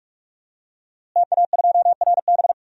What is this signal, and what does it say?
Morse code, heard as a keyed tone, about 44 wpm, 700 Hz. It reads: TA2RB
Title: DXCC-100 Call Signs - 5 Characters - Repeated 2x - 45wpm